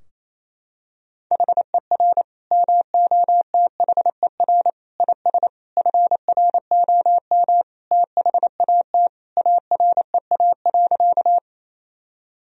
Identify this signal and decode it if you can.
Morse code, heard as a keyed tone, about 28 wpm, 700 Hz. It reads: HER MOTHER IS FROM THAT AREA.
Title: Sentences from Top 1000 Words - Review Part 2 - 28wpm